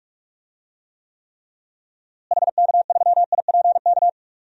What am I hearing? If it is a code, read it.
Morse code, heard as a keyed tone, about 41 wpm, 700 Hz. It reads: HK3IPK
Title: DXCC-100 Call Signs - 6 Characters - Repeated 3x - 45wpm